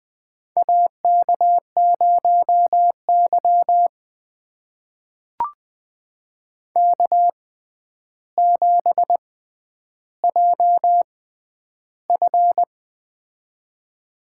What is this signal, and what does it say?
Morse code, heard as a keyed tone, about 20 wpm, spaced as for 10 wpm, 700 Hz. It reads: AK0Y K7JF
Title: USA Call Signs - 4 Characters - 6x Character Spacing - 20wpm